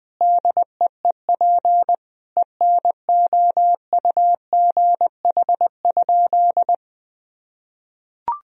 Morse code, heard as a keyed tone, about 20 wpm, 700 Hz. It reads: DEEP ENOUGH?
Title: Sentences from Top 1000 Words - Review Part 3 - 20wpm